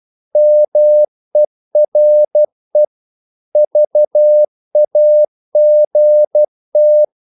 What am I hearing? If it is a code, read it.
Morse code, heard as a keyed tone, about 12 wpm, 600 Hz. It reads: MERE VAGT
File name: Korte ord DK 12wpm